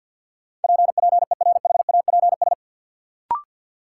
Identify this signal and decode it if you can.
Morse code, heard as a keyed tone, about 53 wpm, 700 Hz. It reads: PPERHAPS
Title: Sentences from Top 1100 Words - New Word Drill Alt - 50wpm